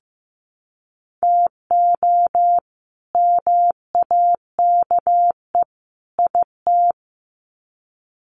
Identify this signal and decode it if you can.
Morse code, heard as a keyed tone, about 15 wpm, 700 Hz. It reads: TO MAKE IT
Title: Sets of 3 Words 15wpm